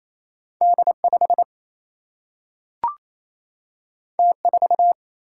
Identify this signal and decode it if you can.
Morse code, heard as a keyed tone, about 28 wpm, 700 Hz. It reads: D5 T4
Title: Single-Letter Single-Number 28wpm